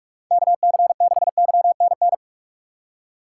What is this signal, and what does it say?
Morse code, heard as a keyed tone, about 42 wpm, 690 Hz. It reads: KC6YNN
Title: USA Call Signs - 6 Characters - 2x Character Spacing - 45wpm